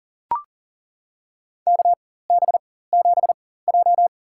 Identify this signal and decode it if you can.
Morse code, heard as a keyed tone, about 40 wpm, spaced as for 19 wpm, 700 Hz. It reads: KB7J
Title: USA Call Signs - 4 Characters - 4x Character Spacing - 40wpm